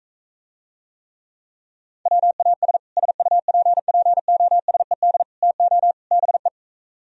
Morse code, heard as a keyed tone, about 45 wpm, 700 Hz. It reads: WAS SUPPOSED TO BE